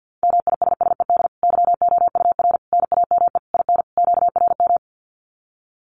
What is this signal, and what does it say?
Morse code, heard as a keyed tone, about 48 wpm, 700 Hz. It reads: MISSED YOUR NAME IN QRM
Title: Common QSO Phrases 50wpm